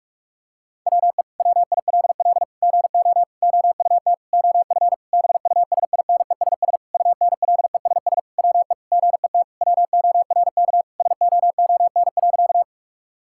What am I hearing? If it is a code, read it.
Morse code, heard as a keyed tone, about 46 wpm, 700 Hz. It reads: WE WILL GO OUT OF BUSINESS UNLESS WE GET WORK SOON.
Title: Sentences from Top 1000 Words - Review Part 6 - 45wpm